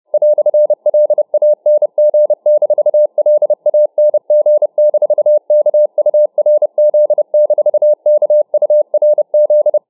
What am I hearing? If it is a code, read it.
Morse code, heard as a keyed tone, about 30 wpm, 600 Hz. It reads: "LANG-LANG-KURZ-KURZ